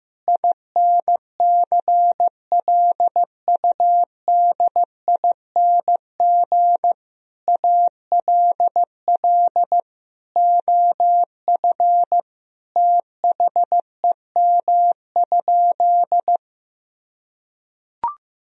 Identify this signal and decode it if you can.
Morse code, heard as a keyed tone, about 15 wpm, 700 Hz. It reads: INCLUDING ALL OF THEM?
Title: Sentences from Top 1000 Words - Review Part 4 - 15wpm